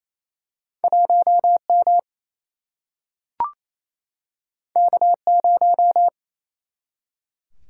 Morse code, heard as a keyed tone, about 28 wpm, 700 Hz. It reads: 1M K0
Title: Single-Letter Single-Number 28wpm